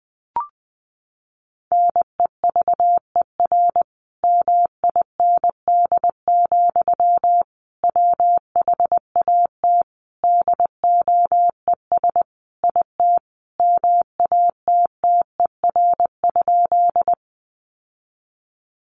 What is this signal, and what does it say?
Morse code, heard as a keyed tone, about 20 wpm, 700 Hz. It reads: NEVER MIND, WHAT DOES IT MATTER?
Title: Sentences from Top 700 Words 20wpm